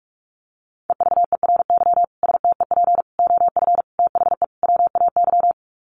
Morse code, heard as a keyed tone, about 44 wpm, 700 Hz. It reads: EVERY STEP OF THE WAY
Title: Sets of 5 Words Encore2 45wpm